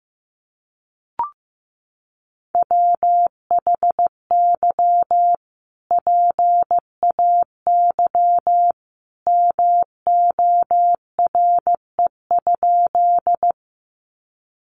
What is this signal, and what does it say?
Morse code, heard as a keyed tone, about 15 wpm, 700 Hz. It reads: WHY PAY MORE?
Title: Sentences from Top 700 Words 15wpm